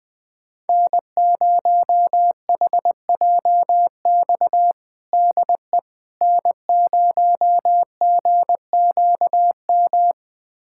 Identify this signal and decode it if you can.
Morse code, heard as a keyed tone, about 21 wpm, 700 Hz. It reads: N0HJX DE N0GQM
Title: USA Callsign DE Callsign - 5 Characters - 20wpm